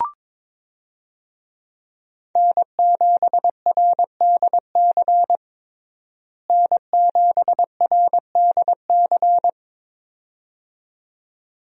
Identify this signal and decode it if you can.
Morse code, heard as a keyed tone, about 22 wpm, 700 Hz. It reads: N7RDC N7RDC
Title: USA Call Signs - 5 Characters - Repeated 2x - 22wpm